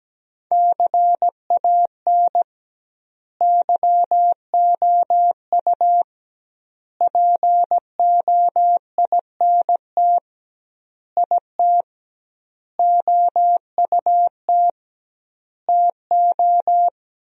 Morse code, heard as a keyed tone, about 17 wpm, 700 Hz. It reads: CAN YOU POINT IT OUT TO
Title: 2x Word-Spacing - Sentences from Top 200 Words - 17wpm